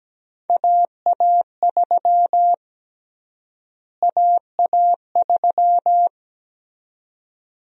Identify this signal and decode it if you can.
Morse code, heard as a keyed tone, about 17 wpm, 700 Hz. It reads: AA3 AA3
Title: USA Call Sign Prefixes - Repeated 2x - 17wpm